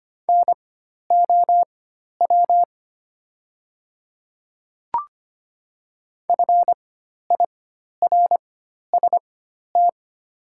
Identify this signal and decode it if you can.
Morse code, heard as a keyed tone, about 25 wpm, spaced as for 12 wpm, 700 Hz. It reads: NOW FIRST
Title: 4x Character Spacing - Top 100 Words - 25wpm